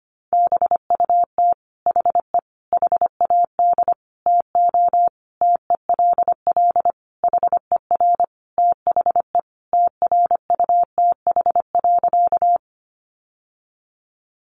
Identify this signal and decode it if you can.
Morse code, heard as a keyed tone, about 25 wpm, 700 Hz. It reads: BUT HE HAD TO TELL HER THE TRUTH.
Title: Sentences from Top 800 Words 25wpm